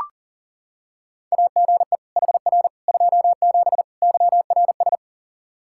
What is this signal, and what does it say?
Morse code, heard as a keyed tone, about 40 wpm, 700 Hz. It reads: AGE HR 27 YRS
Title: Practice QSO - Exchange 3 - More Info - 40wpm